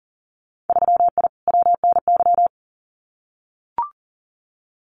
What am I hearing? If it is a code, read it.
Morse code, heard as a keyed tone, about 40 wpm, 700 Hz. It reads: 3I WNY
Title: Winter Field Day Exchanges 40wpm